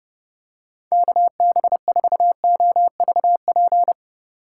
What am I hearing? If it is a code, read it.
Morse code, heard as a keyed tone, about 30 wpm, 700 Hz. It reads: KB4OVP
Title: USA Call Signs - 6 Characters - 2x Character Spacing - 30wpm